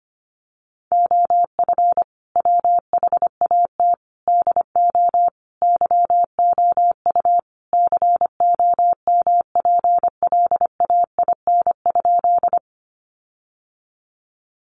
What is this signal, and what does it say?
Morse code, heard as a keyed tone, about 25 wpm, 700 Hz. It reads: OF WHAT DO YOU COMPLAIN?
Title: Sentences from Top 1300 Words 25wpm